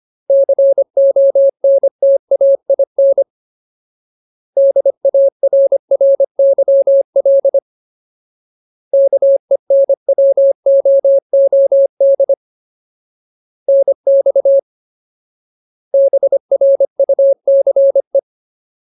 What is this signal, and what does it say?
Morse code, heard as a keyed tone, about 25 wpm, 550 Hz. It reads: CONTAIN DARRYL KENWOOD NX BRUCE
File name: Ultimate Word List / 25WPM Ultimate